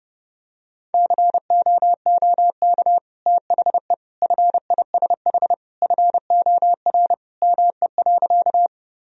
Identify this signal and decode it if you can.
Morse code, heard as a keyed tone, about 32 wpm, 700 Hz. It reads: COOK THE FISH FOR ME.
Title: Sentences from Top 1000 Words - Review Part 4 - 30wpm